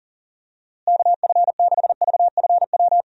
Morse code, heard as a keyed tone, about 40 wpm, 700 Hz. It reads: KF6VFW
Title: USA Call Signs - 6 Characters - Repeated 3x - 40wpm